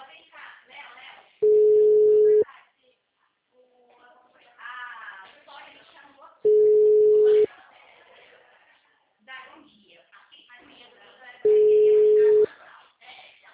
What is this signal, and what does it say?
Ringback tone (Europe, 425 Hz), with speech.